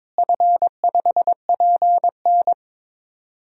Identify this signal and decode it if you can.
Morse code, heard as a keyed tone, about 22 wpm, 700 Hz. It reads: F5PN
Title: DXCC-100 Call Signs - 4 Characters - Repeated 2x - 22wpm